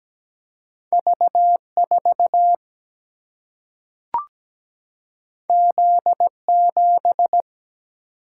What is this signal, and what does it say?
Morse code, heard as a keyed tone, about 17 wpm, 700 Hz. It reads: V4 Z7
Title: Single-Letter Single-Number 17wpm